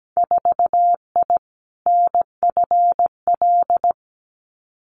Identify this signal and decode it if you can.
Morse code, heard as a keyed tone, about 17 wpm, 700 Hz. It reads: 4I NFL
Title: Winter Field Day Exchanges Encore 17wpm